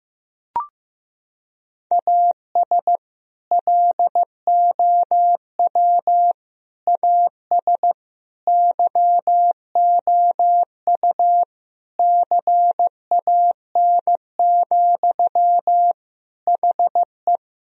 Morse code, heard as a keyed tone, about 15 wpm, 700 Hz. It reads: AS LOW AS YOU CAN, HE
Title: Sentences from Top 1000 Words - Review Part 1 - 15wpm